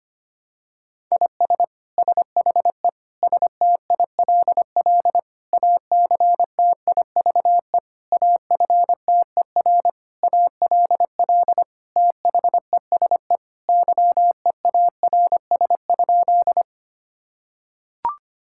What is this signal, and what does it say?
Morse code, heard as a keyed tone, about 25 wpm, 700 Hz. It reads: IS SHE STILL ACTIVE AFTER ALL THESE YEARS?
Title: Sentences from Top 1000 Words - Review Part 6 - 25wpm